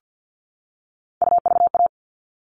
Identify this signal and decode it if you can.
Morse code, heard as a keyed tone, about 50 wpm, 700 Hz. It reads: UVA
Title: All Cut Numbers - 3-Digit 50wpm